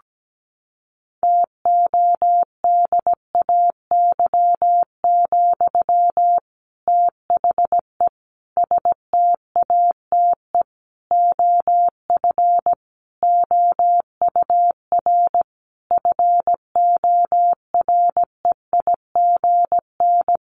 Morse code, heard as a keyed tone, about 17 wpm, 700 Hz. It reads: TODAY, THE STATE OF OUR FOREIGN
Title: Sentences from Top 1400 Words 17wpm